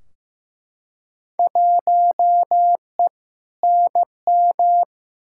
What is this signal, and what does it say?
Morse code, heard as a keyed tone, about 15 wpm, 700 Hz. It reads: 1E NM
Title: Field Day Exchanges - 15wpm